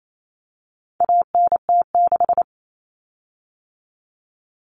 Morse code, heard as a keyed tone, about 28 wpm, 700 Hz. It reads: ANT6
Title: Common Cut Numbers - 4-Digit 28wpm